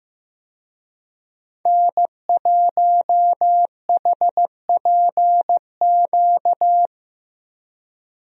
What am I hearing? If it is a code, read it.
Morse code, heard as a keyed tone, about 15 wpm, 700 Hz. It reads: N1HPQ
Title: USA Call Signs - 5 Characters - Repeated 2x - 15wpm